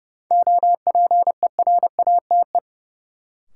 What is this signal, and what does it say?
Morse code, heard as a keyed tone, about 32 wpm, 700 Hz. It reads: OPERATE
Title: Sentences from Top 1500 Words - New Word Drill - 30wpm